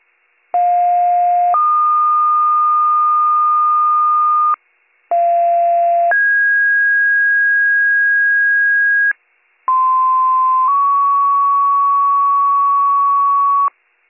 Two-tone page, heard as a one-second tone, then a three-second tone, about 710 then 1220 Hz, about 690 then 1650 Hz, about 1040 then 1120 Hz. It is heard from 0.5 to 13.7 s.